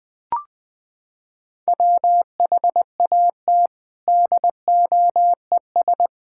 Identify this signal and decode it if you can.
Morse code, heard as a keyed tone, about 20 wpm, 700 Hz. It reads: WHAT DOES
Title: Sets of 9 Words - 20wpm